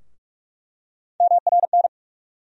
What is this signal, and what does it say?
Morse code, heard as a keyed tone, about 41 wpm, 700 Hz. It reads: MRN
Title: USA Call Sign Postfixes - 3 Letters 45wpm